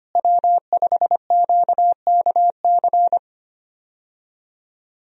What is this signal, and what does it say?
Morse code, heard as a keyed tone, about 25 wpm, 700 Hz. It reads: W5QKC
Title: USA Call Signs - 5 Characters - Repeated 3x - 25wpm